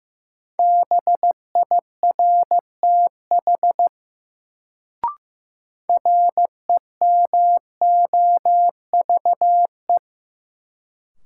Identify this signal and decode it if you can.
Morse code, heard as a keyed tone, about 15 wpm, 700 Hz. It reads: BIRTH REMOVE
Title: Sentences from Top 800 Words - New Word Drill - 15wpm